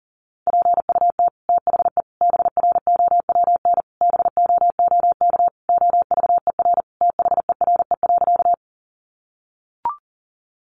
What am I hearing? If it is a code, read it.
Morse code, heard as a keyed tone, about 40 wpm, 700 Hz. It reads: PUT THE BROWN BOOK OVER THERE.
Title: Sentences from Top 1200 Words 40wpm